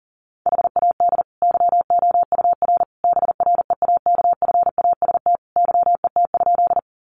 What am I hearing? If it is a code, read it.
Morse code, heard as a keyed tone, about 40 wpm, 700 Hz. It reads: HAD YOUR BREAKFAST YET?
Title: Sentences from Top 1500 Words 40wpm